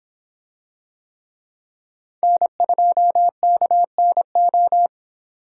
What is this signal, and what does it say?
Morse code, heard as a keyed tone, about 24 wpm, 700 Hz. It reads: N2KNO